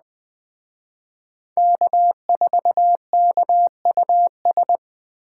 Morse code, heard as a keyed tone, about 21 wpm, 700 Hz. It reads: K4KUS
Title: Speed Racing - USA 1x3 Call Signs 17wpm